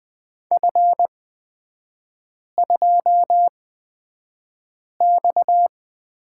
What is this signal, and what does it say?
Morse code, heard as a keyed tone, about 20 wpm, spaced as for 5 wpm, 690 Hz. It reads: F2X
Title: 229 Lesson 37e - Letter x - Three Characters 20wpm